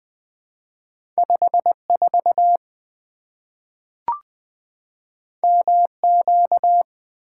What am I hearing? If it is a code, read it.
Morse code, heard as a keyed tone, about 20 wpm, 700 Hz. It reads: 54 MQ
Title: Two Letter-Number Combinations - 20wpm